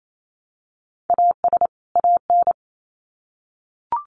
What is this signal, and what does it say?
Morse code, heard as a keyed tone, about 28 wpm, 700 Hz. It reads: AS AN
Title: Sets of 2 Words 28wpm